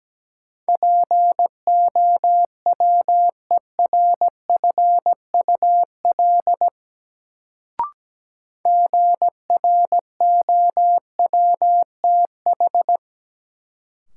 Morse code, heard as a keyed tone, about 17 wpm, 700 Hz. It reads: POWERFUL GROWTH
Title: Sentences from Top 1300 Words - New Word Drill Alt - 17wpm